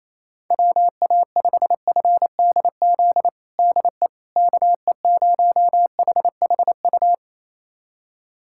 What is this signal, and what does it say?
Morse code, heard as a keyed tone, about 28 wpm, 700 Hz. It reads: WA5FDZ DE KE0HHU
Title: USA Callsign DE Callsign 28wpm